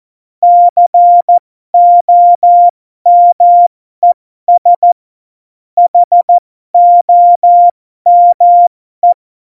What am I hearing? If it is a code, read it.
Morse code, heard as a keyed tone, about 14 wpm, spaced as for 12 wpm, 700 Hz. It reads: COMES HOME
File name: Morsekurs / Witze